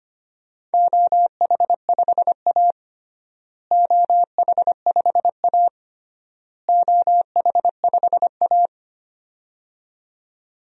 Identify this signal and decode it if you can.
Morse code, heard as a keyed tone, about 25 wpm, 700 Hz. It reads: OH5A OH5A OH5A
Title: DXCC-100 Call Signs - 4 Characters - Repeated 3x - 25wpm